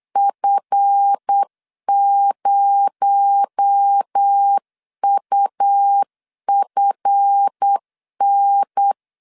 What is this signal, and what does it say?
Morse code, heard as a keyed tone, about 8 wpm, 800 Hz. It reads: F0UFN
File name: LEZIONI / 7 GF9Z / 7 GF 8wpm 3108